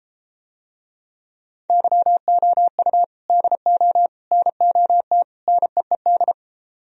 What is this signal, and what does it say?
Morse code, heard as a keyed tone, about 32 wpm, 700 Hz. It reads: YOU DO NOT NEED